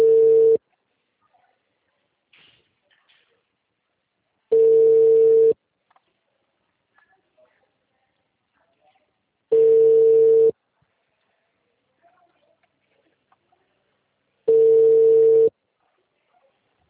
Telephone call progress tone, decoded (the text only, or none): ringback tone